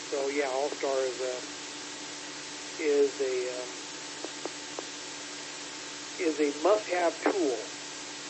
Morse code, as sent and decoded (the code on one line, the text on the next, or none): none